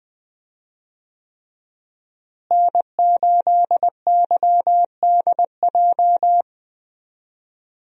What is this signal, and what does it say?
Morse code, heard as a keyed tone, about 20 wpm, 700 Hz. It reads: N8YDJ
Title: USA Call Signs - 5 Characters - Repeated 3x - 20wpm